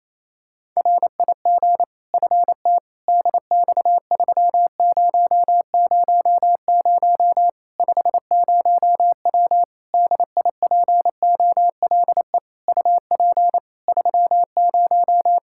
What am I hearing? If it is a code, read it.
Morse code, heard as a keyed tone, about 28 wpm, 700 Hz. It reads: RIG FT DX3000 50W DIPOLE UP 30
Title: Practice QSO - Exchange 2 - Info - 28wpm